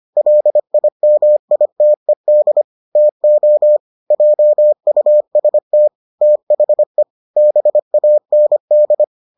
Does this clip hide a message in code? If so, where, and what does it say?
Yes: Morse code from 0.2 to 9.0 s, heard as a keyed tone, about 25 wpm, 600 Hz. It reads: LIMITED TO JUST THE BAND